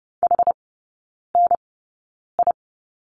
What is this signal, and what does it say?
Morse code, heard as a keyed tone, about 30 wpm, spaced as for 9 wpm, 690 Hz. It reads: HNI
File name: Lesson 17e - Letter u - Three Characters 30wpm